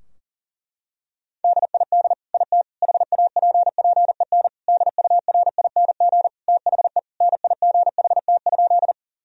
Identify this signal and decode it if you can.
Morse code, heard as a keyed tone, about 40 wpm, 700 Hz. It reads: DID IT HAPPEN DURING THE NIGHT?
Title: Sentences from Top 1000 Words - Review Part 2 - 40wpm